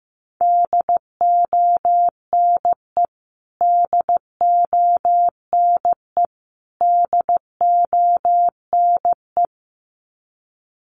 Morse code, heard as a keyed tone, about 15 wpm, 700 Hz. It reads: DONE DONE DONE
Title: Top 500 Words Repeated 15wpm